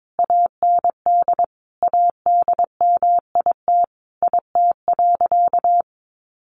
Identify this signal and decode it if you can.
Morse code, heard as a keyed tone, about 22 wpm, 700 Hz. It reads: AND ADMIT IT.
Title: A Sound Of Thunder 22wpm